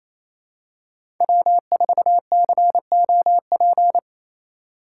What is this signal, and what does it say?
Morse code, heard as a keyed tone, about 28 wpm, 700 Hz. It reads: W4COP
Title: USA Call Signs - 5 Characters - 4x Character Spacing - 28wpm